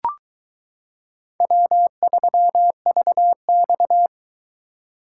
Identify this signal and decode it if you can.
Morse code, heard as a keyed tone, about 20 wpm, 700 Hz. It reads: W3VX